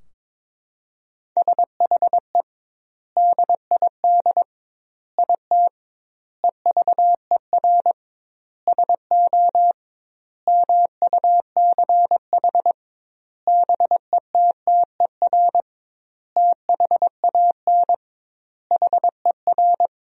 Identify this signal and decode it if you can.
Morse code, heard as a keyed tone, about 21 wpm, 700 Hz. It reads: SHE DID IT EVER SO MUCH BETTER THAN HER
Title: 2x Word-Spacing - Sentences from Top 400 Words - 22wpm